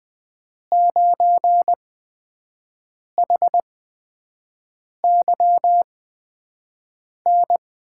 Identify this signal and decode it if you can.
Morse code, heard as a keyed tone, about 20 wpm, spaced as for 5 wpm, 700 Hz. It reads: 9HYN
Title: USA Call Signs - 6 Characters - 8x Character Spacing - 20wpm